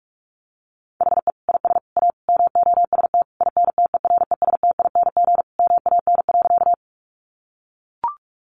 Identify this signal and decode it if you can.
Morse code, heard as a keyed tone, about 44 wpm, 700 Hz. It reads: HE IS A MOST INTERESTING MAN.
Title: Sentences from Top 1000 Words 45wpm